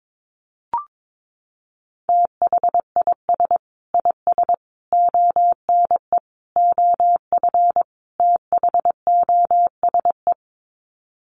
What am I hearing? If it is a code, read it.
Morse code, heard as a keyed tone, about 22 wpm, 700 Hz. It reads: THIS IS ONE OF THOSE
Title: Sets of 5 Words Encore2 22wpm